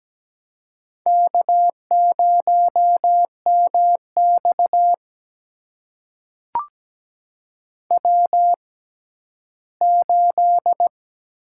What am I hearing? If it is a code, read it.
Morse code, heard as a keyed tone, about 17 wpm, spaced as for 12 wpm, 700 Hz. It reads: K0MX W8
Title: USA Call Signs - 4 Characters - 6x Character Spacing - 17wpm